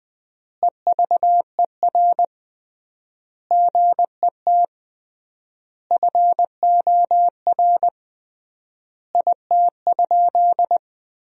Morse code, heard as a keyed tone, about 20 wpm, 700 Hz. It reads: EVER GET FOR IT?
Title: 3x Word-Spacing - Sentences from Top 400 Words - 20wpm